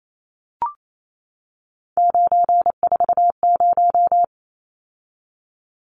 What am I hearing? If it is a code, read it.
Morse code, heard as a keyed tone, about 28 wpm, 700 Hz. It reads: 940
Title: Numbers - 3-Digit 28wpm